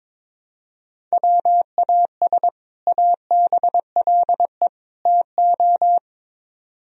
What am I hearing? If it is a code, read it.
Morse code, heard as a keyed tone, about 24 wpm, 700 Hz. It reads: WAS ABLE TO
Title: Speed-Racing - Sets of 3 Words 22wpm